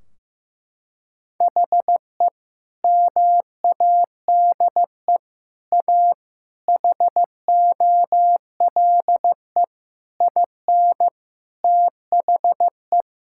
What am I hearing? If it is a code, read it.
Morse code, heard as a keyed tone, about 15 wpm, 700 Hz. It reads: HE MADE A HOLE IN THE
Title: Sentences from Top 1600 Words 15wpm